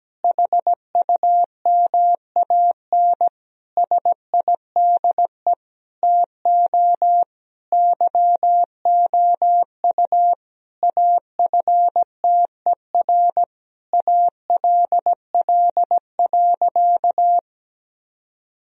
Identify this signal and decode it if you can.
Morse code, heard as a keyed tone, about 17 wpm, 700 Hz. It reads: HUMAN SIDE TO YOU AFTER ALL.
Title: Sentences from Top 1000 Words - Review Part 3 - 17wpm